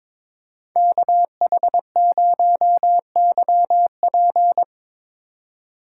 Morse code, heard as a keyed tone, about 21 wpm, 700 Hz. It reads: KH0YP
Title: USA Call Signs - 5 Characters - 3x Character Spacing - 22wpm